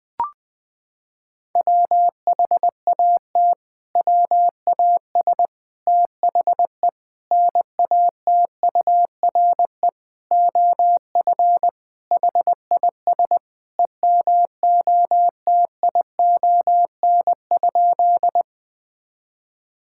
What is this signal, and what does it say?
Morse code, heard as a keyed tone, about 20 wpm, 700 Hz. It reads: WHAT WAS THE NATURE OF HIS EMOTION?
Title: Sentences from Top 1600 Words 20wpm